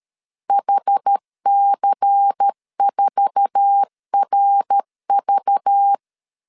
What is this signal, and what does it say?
Morse code, heard as a keyed tone, about 13 wpm, 800 Hz. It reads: HC4RV
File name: CW / DET / 1 9 1500C S A 12wpm 3800